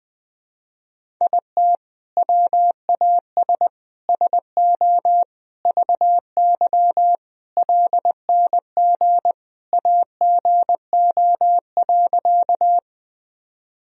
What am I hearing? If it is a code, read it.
Morse code, heard as a keyed tone, about 21 wpm, 700 Hz. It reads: IT WAS SO VY LNG AGO.
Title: Sentences from Top 500 Words and Abbreviations 20wpm